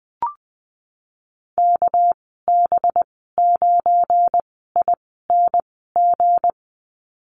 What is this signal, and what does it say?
Morse code, heard as a keyed tone, about 19 wpm, 700 Hz. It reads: K B 9 I N G
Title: 238 Lesson 38g - Letter q - Call Signs 20wpm